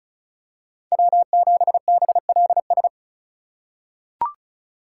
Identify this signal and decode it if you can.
Morse code, heard as a keyed tone, about 36 wpm, 700 Hz. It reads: W7BLS
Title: USA Call Signs - 5 Characters - 3x Character Spacing - 35wpm